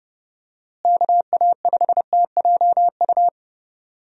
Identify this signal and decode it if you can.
Morse code, heard as a keyed tone, about 30 wpm, 700 Hz. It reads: KA5TJU
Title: USA Call Signs - 6 Characters - Repeated 2x - 30wpm